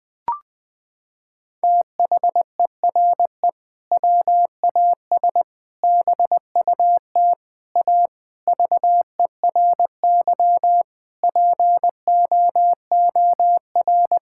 Morse code, heard as a keyed tone, about 21 wpm, 700 Hz. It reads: THERE WAS BUT A VERY POOR
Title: Sentences from Top 1000 Words - Review Part 5 - 20wpm